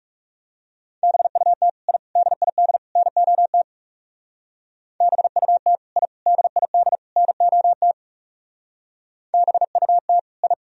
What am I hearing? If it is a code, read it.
Morse code, heard as a keyed tone, about 36 wpm, 700 Hz. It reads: BUT I DID NOT BUT I DID NOT BUT I
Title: Speed-Racing - Sets of 4 Words 30wpm